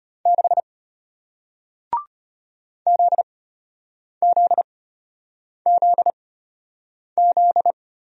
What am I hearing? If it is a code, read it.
Morse code, heard as a keyed tone, about 33 wpm, spaced as for 8 wpm, 700 Hz. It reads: 6 ZZZZ